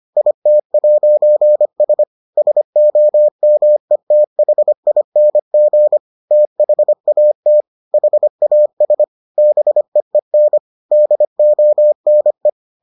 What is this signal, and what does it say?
Morse code, heard as a keyed tone, about 25 wpm, 600 Hz. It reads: IT'S SOMETHING THAT HAS BEEN DONE